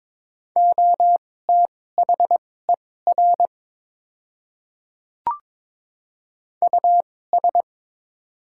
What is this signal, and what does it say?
Morse code, heard as a keyed tone, about 23 wpm, spaced as for 16 wpm, 700 Hz. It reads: OTHER US
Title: 2x Character Spacing - Top 300 Words - 22wpm